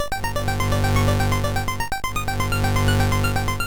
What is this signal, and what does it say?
There is no code.